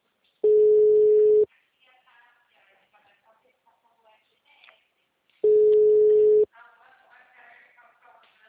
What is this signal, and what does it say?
Ringback tone (Europe, 425 Hz), with speech.